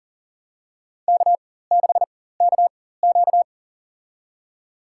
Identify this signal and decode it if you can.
Morse code, heard as a keyed tone, about 40 wpm, spaced as for 19 wpm, 700 Hz. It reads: K6KQ
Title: USA Call Signs - 4 Characters - 4x Character Spacing - 40wpm